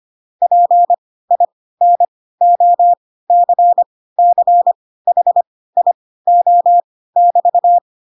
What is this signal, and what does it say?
Morse code, heard as a keyed tone, about 25 wpm, spaced as for 16 wpm, 700 Hz. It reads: PINOCCHIO<BT>
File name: Pinocchio / Pinocchio ch3 10 25 ik1hge